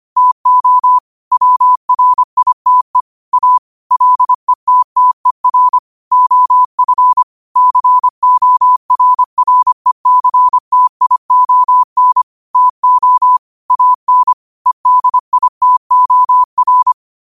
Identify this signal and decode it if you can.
Morse code, heard as a keyed tone, about 25 wpm, 1000 Hz. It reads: TO WRITE A LETTER OF CORRECTION TO AN EDITOR